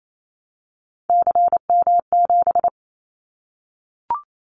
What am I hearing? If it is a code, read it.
Morse code, heard as a keyed tone, about 28 wpm, 700 Hz. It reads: CM7
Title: Callsign Prefixes for Top 100 DXCC - 28wpm